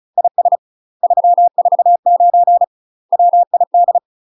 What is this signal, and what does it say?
Morse code, heard as a keyed tone, about 37 wpm, 700 Hz. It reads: IS 349 WID
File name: UK MASTER QSOs TRAINING / 35WPM / UK MASTER QSOs 0008